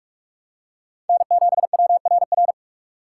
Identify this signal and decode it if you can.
Morse code, heard as a keyed tone, about 46 wpm, 700 Hz. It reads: N7WRR
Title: USA Call Signs - 5 Characters - 3x Character Spacing - 45wpm